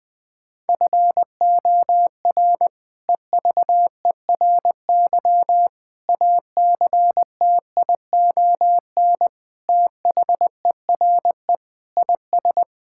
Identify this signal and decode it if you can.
Morse code, heard as a keyed tone, about 20 wpm, 700 Hz. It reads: FOR EVERY ACTION THERE IS
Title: Sets of 8 Words - 20wpm